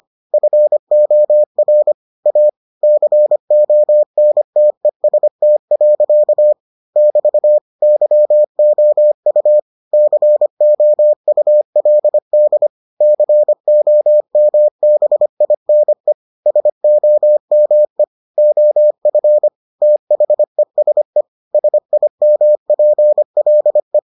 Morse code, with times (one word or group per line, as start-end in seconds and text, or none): 0.3-1.9 FOR
2.3-2.5 A
2.8-6.5 CONTEST.
7.0-7.6 <BT>
7.8-9.6 YOU
9.9-12.7 COULD
13.0-16.1 COMBINE
16.5-18.0 SOME
18.4-19.5 OF
19.8-21.2 THESE
21.5-24.0 SIMPLE